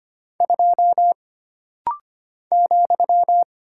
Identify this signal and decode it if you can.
Morse code, heard as a keyed tone, about 25 wpm, 700 Hz. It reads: 2 ,
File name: Single Character (Letter:Number:Punctuation:Prosign) - Rapid-Fire - 25wpm